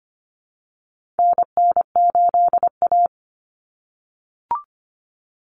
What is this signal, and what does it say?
Morse code, heard as a keyed tone, about 25 wpm, 700 Hz. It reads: NN8A
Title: Common Cut Numbers - 4-Digit 25wpm